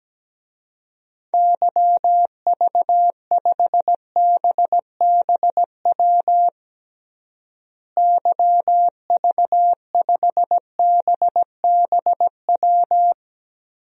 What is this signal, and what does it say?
Morse code, heard as a keyed tone, about 17 wpm, 700 Hz. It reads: YV5BBW YV5BBW
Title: DXCC-100 Call Signs - 6 Characters - Repeated 3x - 17wpm